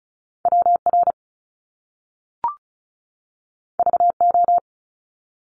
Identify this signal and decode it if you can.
Morse code, heard as a keyed tone, about 36 wpm, 700 Hz. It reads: WR VO
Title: Top Two Letter Combinations in Words 35wpm